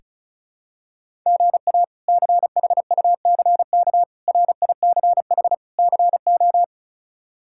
Morse code, heard as a keyed tone, about 36 wpm, 700 Hz. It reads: GA CHUCK RICH CO
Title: K1USN SST QSO Practice - 35wpm